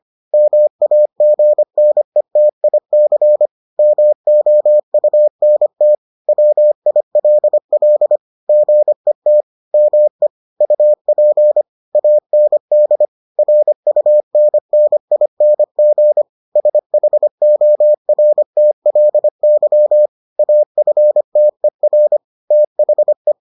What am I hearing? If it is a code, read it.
Morse code, heard as a keyed tone, about 25 wpm, 600 Hz. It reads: MAGNETIC MOUNT WILL GET ME UP AND RUNNING SHORTLY AFTER THE